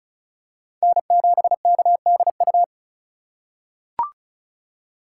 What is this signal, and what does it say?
Morse code, heard as a keyed tone, about 36 wpm, 700 Hz. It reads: N7KDU
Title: USA Call Signs - 5 Characters - 3x Character Spacing - 35wpm